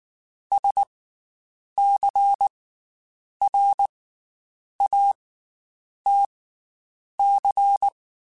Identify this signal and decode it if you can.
Morse code, heard as a keyed tone, about 19 wpm, 800 Hz. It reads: S C R A T C
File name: Special INGs / Special INGs-mp3s / Slow ing @18